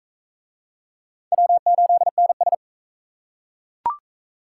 Morse code, heard as a keyed tone, about 40 wpm, 700 Hz. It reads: W8NS